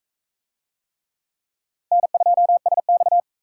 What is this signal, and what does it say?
Morse code, heard as a keyed tone, about 37 wpm, 700 Hz. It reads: N2SX